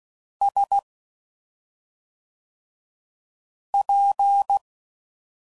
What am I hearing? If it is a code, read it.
Morse code, heard as a keyed tone, about 16 wpm, 800 Hz. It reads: S P
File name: Suffixes / Special ING's / SlowToFaster ING's